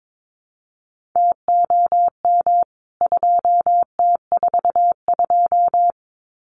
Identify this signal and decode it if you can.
Morse code, heard as a keyed tone, about 22 wpm, 700 Hz. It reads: TOM 2T42
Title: CWT Exchange 22wpm